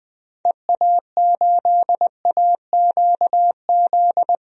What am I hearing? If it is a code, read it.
Morse code, heard as a keyed tone, about 20 wpm, 700 Hz. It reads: EA8AQZ
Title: DXCC-100 Call Signs - 6 Characters - Repeated 2x - 20wpm